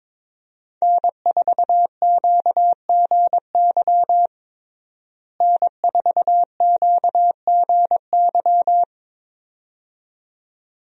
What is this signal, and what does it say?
Morse code, heard as a keyed tone, about 22 wpm, 700 Hz. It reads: N4QGY N4QGY
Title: USA Call Signs - 5 Characters - Repeated 2x - 22wpm